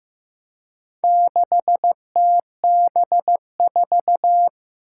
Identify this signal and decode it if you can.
Morse code, heard as a keyed tone, about 15 wpm, 700 Hz. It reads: 6TB4
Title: Four Challenging Letter-Number Combinations 15wpm